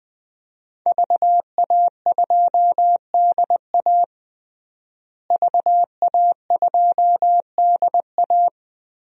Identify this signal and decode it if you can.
Morse code, heard as a keyed tone, about 20 wpm, 700 Hz. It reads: VA2DA VA2DA
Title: DXCC-100 Call Signs - 5 Characters - Repeated 3x - 20wpm